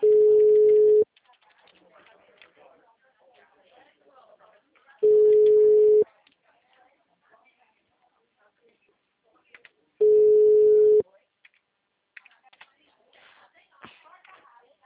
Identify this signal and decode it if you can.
Ringback tone (Europe, 425 Hz), with speech.